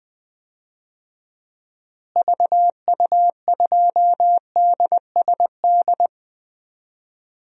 Morse code, heard as a keyed tone, about 20 wpm, 700 Hz. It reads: VU2DSD
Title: DXCC-100 Call Signs - 6 Characters - Repeated 2x - 20wpm